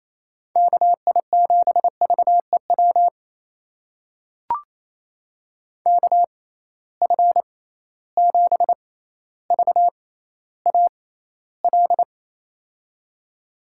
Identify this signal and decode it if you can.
Morse code, heard as a keyed tone, about 28 wpm, spaced as for 14 wpm, 700 Hz. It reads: KI7VEW KF7VAL
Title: USA Call Signs - 6 Characters - 6x Character Spacing - 28wpm